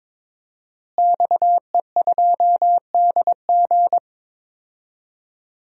Morse code, heard as a keyed tone, about 23 wpm, 700 Hz. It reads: XE2DG
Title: DXCC-100 Call Signs - 5 Characters - Repeated 2x - 22wpm